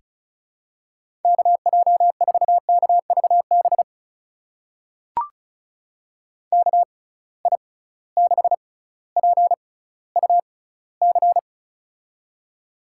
Morse code, heard as a keyed tone, about 36 wpm, spaced as for 18 wpm, 700 Hz. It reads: KJ4KVB KI6PUC
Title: USA Call Signs - 6 Characters - 6x Character Spacing - 35wpm